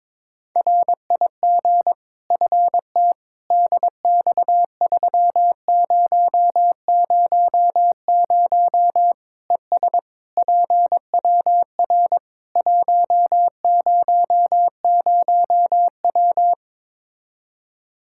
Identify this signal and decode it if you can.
Morse code, heard as a keyed tone, about 22 wpm, 700 Hz. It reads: RIG FT DX3000 ES PWR 100W
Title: Practice QSO - Exchange 2 - Info - 22wpm